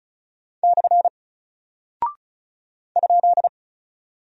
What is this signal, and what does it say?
Morse code, heard as a keyed tone, about 36 wpm, 700 Hz. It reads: / ?
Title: Single Character (Letter:Number:Punctuation:Prosign) Alt 35wpm